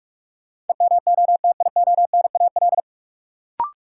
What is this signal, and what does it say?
Morse code, heard as a keyed tone, about 41 wpm, 700 Hz. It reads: EMOTIONAL
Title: Sentences from Top 1000 Words - Review Part 5 - New Word Drill - 45wpm